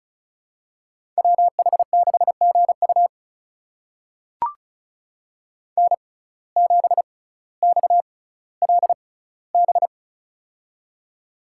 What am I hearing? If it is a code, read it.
Morse code, heard as a keyed tone, about 36 wpm, spaced as for 18 wpm, 700 Hz. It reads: WH6GU N7XLB
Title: USA Call Signs - 5 Characters - 6x Character Spacing - 35wpm